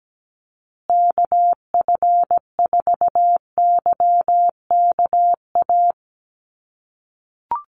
Morse code, heard as a keyed tone, about 17 wpm, 700 Hz. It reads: KF4YKA
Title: USA Call Signs Repeated 2x 17wpm